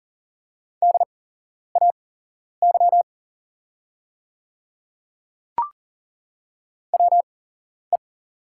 Morse code, heard as a keyed tone, about 40 wpm, spaced as for 11 wpm, 690 Hz. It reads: DAY WE
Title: 8x Character Spacing - Top 200 Words - 40wpm